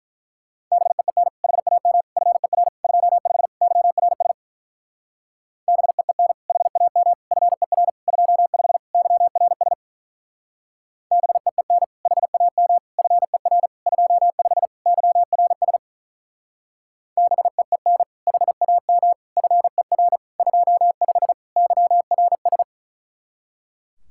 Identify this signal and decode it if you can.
Morse code, heard as a keyed tone, about 44 wpm, 700 Hz. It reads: BEEN HAM FER 25 YRS BEEN HAM FER 25 YRS BEEN HAM FER 25 YRS BEEN HAM FER 25 YRS